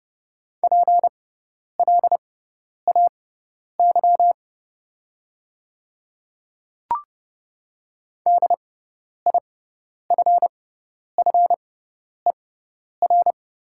Morse code, heard as a keyed tone, about 30 wpm, spaced as for 10 wpm, 690 Hz. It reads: PLAY DIFFER
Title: 6x Character Spacing - Top 300 Words - 30wpm